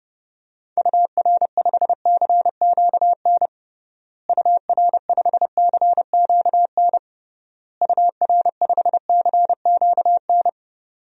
Morse code, heard as a keyed tone, about 30 wpm, 700 Hz. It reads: UR5CQN UR5CQN UR5CQN
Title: DXCC-100 Call Signs - 6 Characters - Repeated 3x - 30wpm